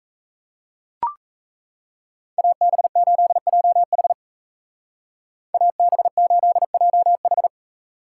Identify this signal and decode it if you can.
Morse code, heard as a keyed tone, about 34 wpm, 700 Hz. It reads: AB8JH AB8JH
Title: Speed Racing - USA 2x2 Call Signs 28wpm